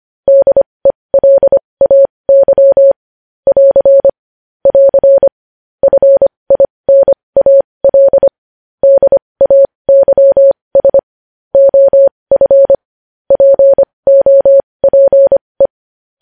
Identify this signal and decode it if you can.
Morse code, heard as a keyed tone, about 25 wpm, spaced as for 20 wpm, 550 Hz. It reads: DELAY <AR> <AR> FINAL DAYS OF POPE
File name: podcast / News headlines-1x-25x20